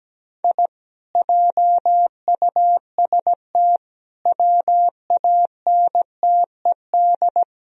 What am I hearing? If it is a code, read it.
Morse code, heard as a keyed tone, about 18 wpm, 700 Hz. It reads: I JUST WANTED
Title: Speed-Racing - Sets of 4 Words 17wpm